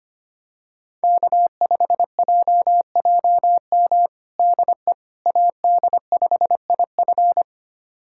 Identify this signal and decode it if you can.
Morse code, heard as a keyed tone, about 25 wpm, 700 Hz. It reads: K5JJM DE AD5IF
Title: USA Callsign DE Callsign - 5 Characters - 25wpm